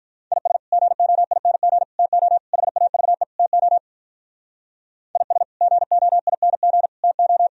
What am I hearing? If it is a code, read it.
Morse code, heard as a keyed tone, about 43 wpm, 700 Hz. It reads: IS GOING TO HAVE TO IS GOING TO
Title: Speed-Racing - Sets of 5 Words 35wpm